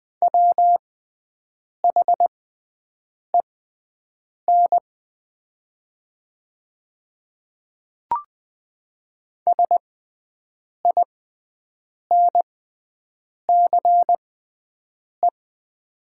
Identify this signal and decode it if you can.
Morse code, heard as a keyed tone, about 20 wpm, spaced as for 7 wpm, 700 Hz. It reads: WHEN SINCE
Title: 6x Character Spacing - Top 500 Words - 20wpm